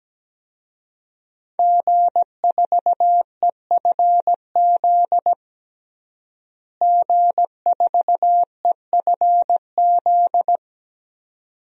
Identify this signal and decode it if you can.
Morse code, heard as a keyed tone, about 17 wpm, 700 Hz. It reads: G4EFZ G4EFZ
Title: DXCC-100 Call Signs - 5 Characters - Repeated 2x - 17wpm